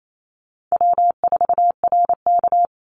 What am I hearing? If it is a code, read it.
Morse code, heard as a keyed tone, about 28 wpm, 700 Hz. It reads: W4RK
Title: NA SOTA Chaser Callsigns 28wpm